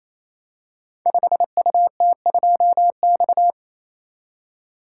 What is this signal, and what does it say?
Morse code, heard as a keyed tone, about 27 wpm, 700 Hz. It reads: 5UT2X
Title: Code Groups - 5 Letter:Numbers - 28wpm